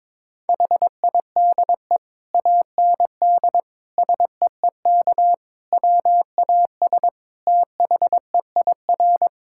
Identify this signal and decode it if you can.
Morse code, heard as a keyed tone, about 22 wpm, 700 Hz. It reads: HIDE AND SEEK WAS THEIR
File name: Difficult Sentences with Hints - 22wpm